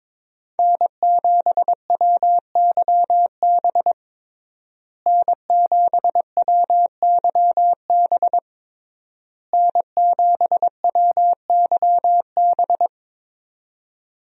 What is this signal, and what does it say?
Morse code, heard as a keyed tone, about 22 wpm, 700 Hz. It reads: N7WYB N7WYB N7WYB
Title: USA Call Signs - 5 Characters - Repeated 3x - 22wpm